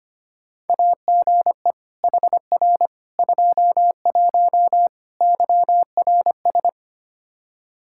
Morse code, heard as a keyed tone, about 25 wpm, 700 Hz. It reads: AGE HR 21 YRS
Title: K1USN SST QSO Practice 25wpm